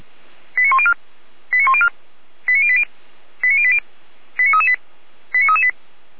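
Five-tone selective call (ZVEI), heard as quick runs of five tones, tones of about 70 ms. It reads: 89104 89104 89080 89080 89338 89338
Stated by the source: ZVEI five-tone call 89208 received by SDR.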